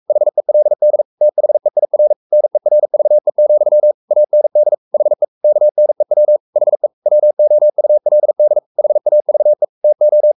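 Morse code, heard as a keyed tone, about 44 wpm, 600 Hz. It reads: HELD THEIR NERVE, AND HE KNEW HE WOULD HAVE TO
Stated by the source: Morse code at 43 wpm sending Ken Follett's novel Never in English, from CW med Gnister.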